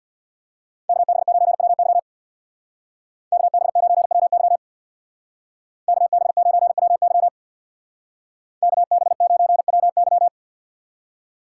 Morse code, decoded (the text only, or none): KB9WY KB9WY KB9WY KB9WY